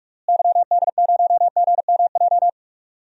Morse code, heard as a keyed tone, about 46 wpm, 700 Hz. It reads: YD0GMJ
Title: DXCC-100 Call Signs - 6 Characters - Repeated 2x - 45wpm